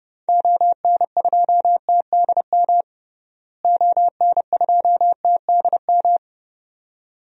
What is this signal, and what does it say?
Morse code, heard as a keyed tone, about 32 wpm, 700 Hz. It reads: ON2TDM ON2TDM
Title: DXCC-100 Call Signs - 6 Characters - Repeated 2x - 30wpm